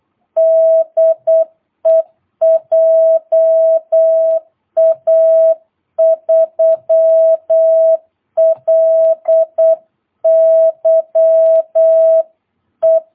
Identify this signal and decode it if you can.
Morse code, heard as a keyed tone, about 8 wpm, 650 Hz. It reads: DE JA3LYE
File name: jh8cbh / 2022cw